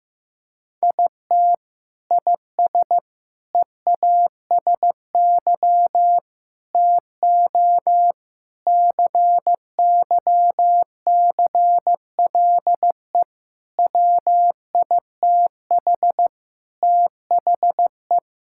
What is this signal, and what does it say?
Morse code, heard as a keyed tone, about 15 wpm, 700 Hz. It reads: IT IS EASY TO CYCLE WITH THE
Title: Sentences from Top 1000 Words - Review Part 6 - 15wpm